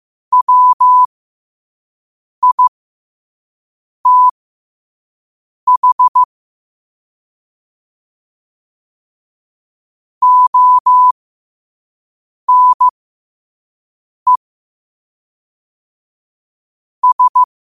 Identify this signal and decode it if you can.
Morse code, heard as a keyed tone, about 15 wpm, spaced as for 5 wpm, 1000 Hz. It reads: WITH ONE S
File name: qotdfiles / 5wpm